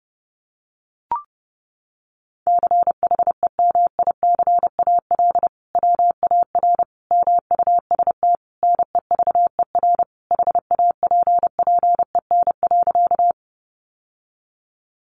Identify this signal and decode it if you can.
Morse code, heard as a keyed tone, about 30 wpm, 700 Hz. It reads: CHEMICAL WAR MUST NEVER HAPPEN.
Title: Sentences from Top 1200 Words 30wpm